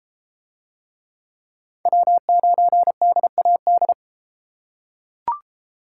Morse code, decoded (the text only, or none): W9DAD